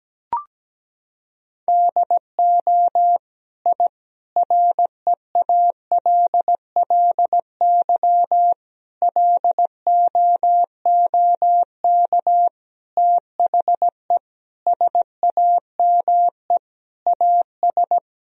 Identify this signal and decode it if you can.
Morse code, heard as a keyed tone, about 17 wpm, 700 Hz. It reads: DO I REALLY LOOK THE SAME AS
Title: Sentences from Top 1000 Words - Review Part 2 - 17wpm